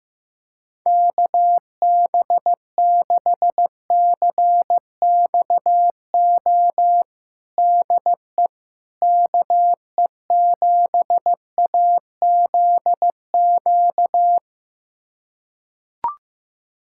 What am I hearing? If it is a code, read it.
Morse code, heard as a keyed tone, about 16 wpm, 700 Hz. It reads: KB6CXO DE KE7AZQ
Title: USA Callsign DE Callsign - 6 Characters - 15wpm